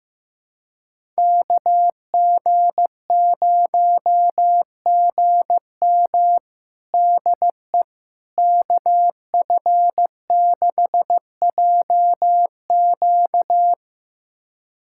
Morse code, heard as a keyed tone, about 15 wpm, 700 Hz. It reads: KG0GM DE KF6JQ
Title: USA Callsign DE Callsign - 5 Characters - 15wpm